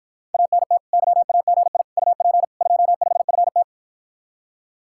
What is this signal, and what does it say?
Morse code, heard as a keyed tone, about 54 wpm, 700 Hz. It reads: ANT YAGI UP 25FT